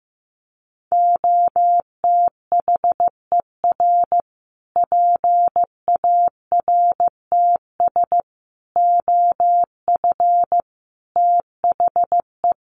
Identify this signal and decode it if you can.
Morse code, heard as a keyed tone, about 15 wpm, 700 Hz. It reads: OTHER PARTS OF THE
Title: Sets of 5 Words Encore 15wpm